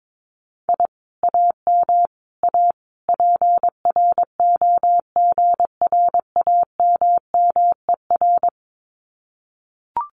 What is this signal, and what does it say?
Morse code, heard as a keyed tone, about 22 wpm, 700 Hz. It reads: I AM A PROGRAMMER
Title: Common QSO Phrases Encore 22wpm